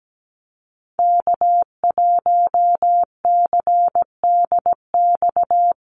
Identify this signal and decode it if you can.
Morse code, heard as a keyed tone, about 17 wpm, 700 Hz. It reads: K1CDX
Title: Call Signs - 17wpm